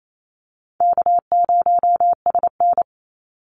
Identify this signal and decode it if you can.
Morse code, heard as a keyed tone, about 28 wpm, 700 Hz. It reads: K0SN
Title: Easy Call Signs - 28wpm